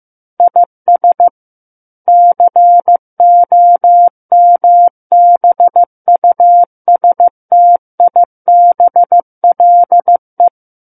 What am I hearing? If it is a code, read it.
Morse code, heard as a keyed tone, about 15 wpm, 700 Hz. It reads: IS COMBUSTIBLE
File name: Chapter 6 - 15 WPM